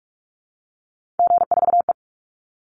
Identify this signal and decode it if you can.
Morse code, heard as a keyed tone, about 44 wpm, 700 Hz. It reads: G4E
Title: Call Signs Repeated 3x - 45wpm